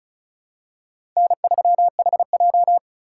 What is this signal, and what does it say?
Morse code, heard as a keyed tone, about 34 wpm, 700 Hz. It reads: N3HJ
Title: USA Call Signs - 4 Characters - 4x Character Spacing - 35wpm